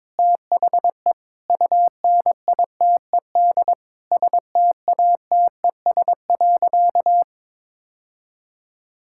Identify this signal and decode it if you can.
Morse code, heard as a keyed tone, about 22 wpm, 700 Hz. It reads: THE UNITED STATES.
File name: practice / 22wpm / Sentences with 3-Word Phrases - 22wpm